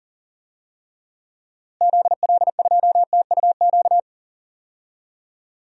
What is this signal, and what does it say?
Morse code, heard as a keyed tone, about 39 wpm, 700 Hz. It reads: ZL2TUQ
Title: DXCC-100 Call Signs - 6 Characters - Repeated 2x - 40wpm